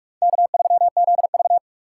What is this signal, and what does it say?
Morse code, heard as a keyed tone, about 44 wpm, 700 Hz. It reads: K3ZV
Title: USA Call Signs - 4 Characters - 3x Character Spacing - 45wpm